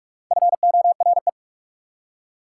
Morse code, heard as a keyed tone, about 44 wpm, 690 Hz. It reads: FORE
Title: Prefixes and Suffixes 45wpm